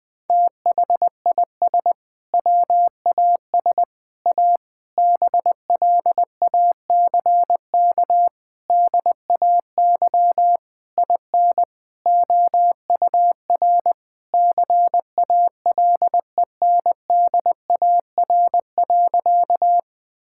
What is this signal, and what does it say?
Morse code, heard as a keyed tone, about 21 wpm, 700 Hz. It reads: THIS WAS A BLACK DAY IN OUR CALENDAR.
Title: Sentences from Top 1600 Words 20wpm